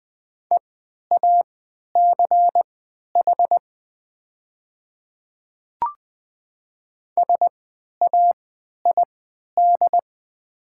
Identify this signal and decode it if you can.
Morse code, heard as a keyed tone, about 20 wpm, spaced as for 11 wpm, 700 Hz. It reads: EACH SAID
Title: 3x Character Spacing - Top 100 Words - 20wpm